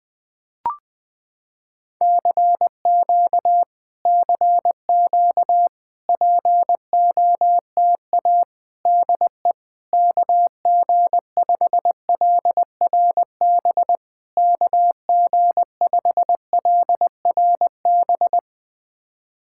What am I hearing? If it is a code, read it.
Morse code, heard as a keyed tone, about 20 wpm, 700 Hz. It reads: CQ CQ POTA DE KG5LRB KG5LRB
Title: POTA QSO Practice - 20wpm